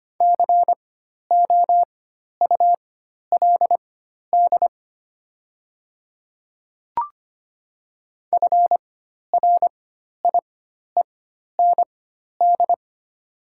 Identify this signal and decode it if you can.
Morse code, heard as a keyed tone, about 25 wpm, spaced as for 12 wpm, 700 Hz. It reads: COULD FRIEND
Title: 4x Character Spacing - Top 500 Words - 25wpm